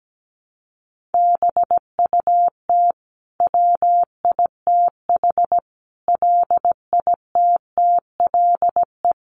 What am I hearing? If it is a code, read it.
Morse code, heard as a keyed tone, about 17 wpm, 700 Hz. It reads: BUT WITH LITTLE
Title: Sentences from Top 1400 Words 17wpm